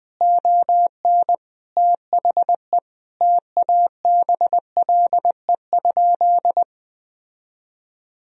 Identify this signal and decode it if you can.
Morse code, heard as a keyed tone, about 20 wpm, 700 Hz. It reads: ON THE TABLE?
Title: Sentences from Top 1000 Words - Review Part 3 - 20wpm